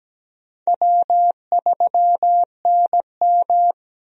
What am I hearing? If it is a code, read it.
Morse code, heard as a keyed tone, about 17 wpm, 700 Hz. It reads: W3NM
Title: USA Call Signs - 4 Characters - Repeated 3x - 17wpm